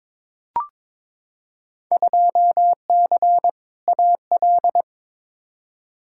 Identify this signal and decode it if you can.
Morse code, heard as a keyed tone, about 22 wpm, 700 Hz. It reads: 2C AL
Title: Field Day Exchanges Encore - 22wpm